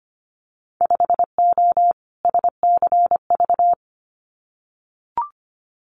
Morse code, heard as a keyed tone, about 25 wpm, 700 Hz. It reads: 5O SCV
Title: Winter Field Day Exchanges Encore 25wpm